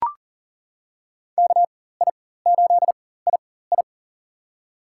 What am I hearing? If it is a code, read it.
Morse code, heard as a keyed tone, about 40 wpm, spaced as for 19 wpm, 700 Hz. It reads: KI8II
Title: USA Call Signs - 5 Characters - 4x Character Spacing - 40wpm